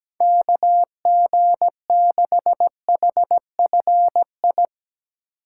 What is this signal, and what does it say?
Morse code, heard as a keyed tone, about 17 wpm, 700 Hz. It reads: KG6HFI
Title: USA Call Signs - 4x Character Spacing - 17wpm